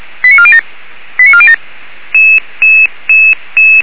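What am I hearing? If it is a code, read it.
Five-tone selective call (ZVEI), heard as quick runs of five tones, tones of about 70 ms. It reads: 89407 89407
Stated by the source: ZVEI five-tone selective call 89406 received by SDR, fire brigade Feuerwehr Orlamünde.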